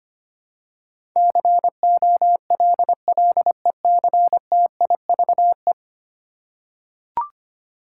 Morse code, heard as a keyed tone, about 25 wpm, 700 Hz. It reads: COLLECTIVE
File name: 10 Letter Words - First 3 Letters Given - 25wpm